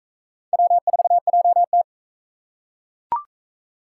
Morse code, heard as a keyed tone, about 41 wpm, 700 Hz. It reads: W4JT